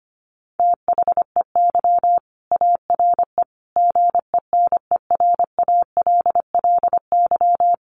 Morse code, heard as a keyed tone, about 24 wpm, 700 Hz. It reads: THEY ARE GENERALLY
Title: Sentences from Top 600 Words 25wpm